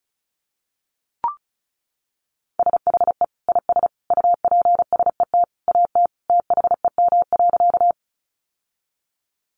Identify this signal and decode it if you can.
Morse code, heard as a keyed tone, about 34 wpm, 700 Hz. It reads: SHE IS UPSET AT THEM.
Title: Sentences from Top 800 Words 35wpm